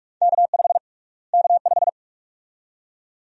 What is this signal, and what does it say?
Morse code, heard as a keyed tone, about 46 wpm, 700 Hz. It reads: K5 K5
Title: USA Call Sign Prefixes - Repeated 2x - 45wpm